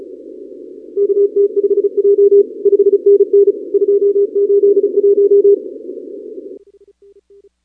Morse code, heard as a keyed tone, about 35 wpm, 410 Hz. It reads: KT5J 5NN 281
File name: contests / 2019-cw-wpx-kt5j-k5pi- -2330 0-38